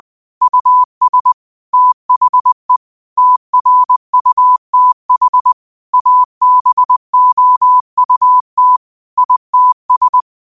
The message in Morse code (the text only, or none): US THE TRUTH ABOUT ITS